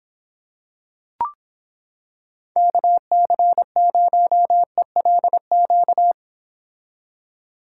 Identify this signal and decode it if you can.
Morse code, heard as a keyed tone, about 26 wpm, 700 Hz. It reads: KC0ELQ